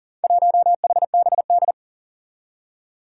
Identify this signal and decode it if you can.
Morse code, heard as a keyed tone, about 43 wpm, 700 Hz. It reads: 1HBD
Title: Four Challenging Letter-Number Combinations 40wpm